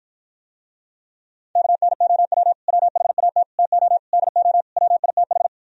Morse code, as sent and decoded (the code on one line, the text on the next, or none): -.- -. --- .-- / .-- .... .- - / - --- / -.. --- / .-- .. - ....
KNOW WHAT TO DO WITH